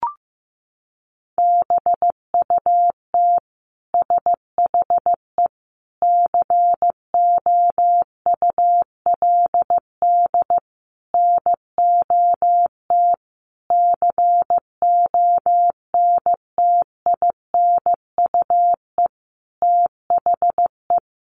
Morse code, heard as a keyed tone, about 15 wpm, 700 Hz. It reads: BUT SHE COULD NOT CONTINUE THE
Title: Sentences from Top 1200 Words 15wpm